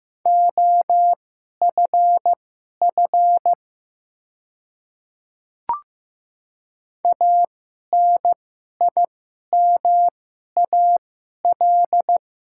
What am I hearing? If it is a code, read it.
Morse code, heard as a keyed tone, about 15 wpm, spaced as for 11 wpm, 700 Hz. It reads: OFF ANIMAL
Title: 2x Character Spacing - Top 200 Words - 15wpm